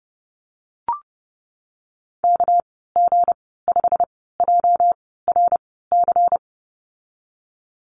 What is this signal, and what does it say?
Morse code, heard as a keyed tone, about 31 wpm, spaced as for 17 wpm, 700 Hz. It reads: KG5JRC
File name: Lesson 37g - Letter x - Call Signs 30wpm